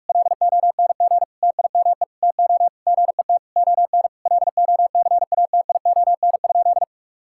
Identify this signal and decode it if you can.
Morse code, heard as a keyed tone, about 46 wpm, 700 Hz. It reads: LONG TIME TO GET ON LOCATION?
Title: Sentences from Top 1000 Words - Review Part 6 - 45wpm